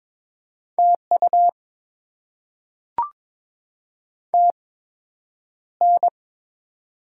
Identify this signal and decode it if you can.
Morse code, heard as a keyed tone, about 22 wpm, spaced as for 9 wpm, 700 Hz. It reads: TU TN
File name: Top 20 QSO Elements - 8x Character Spacing - 22wpm